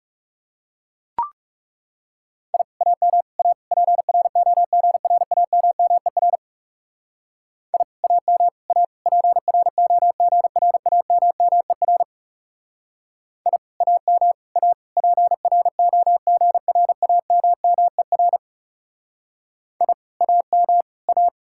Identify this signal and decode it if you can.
Morse code, heard as a keyed tone, about 34 wpm, 700 Hz. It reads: I AM A PROGRAMMER I AM A PROGRAMMER I AM A PROGRAMMER I AM A
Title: Speed-Racing Common QSO Phrases Encore - 30wpm